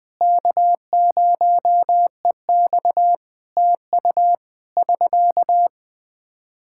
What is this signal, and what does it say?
Morse code, heard as a keyed tone, about 21 wpm, 700 Hz. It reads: K0EX TU <SK>
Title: Practice QSO - Exchange 4 - Ending - 20wpm